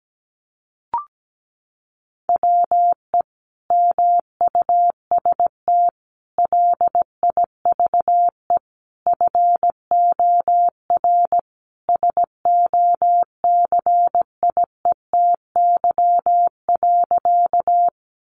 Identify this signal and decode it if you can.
Morse code, heard as a keyed tone, about 17 wpm, 700 Hz. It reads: WE MUST LIVE FOR SOCIETY.
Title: Sentences from Top 700 Words 17wpm